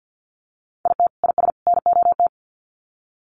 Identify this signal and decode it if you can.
Morse code, heard as a keyed tone, about 50 wpm, 700 Hz. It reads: IT IS NOT
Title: Sets of 3 Words 50wpm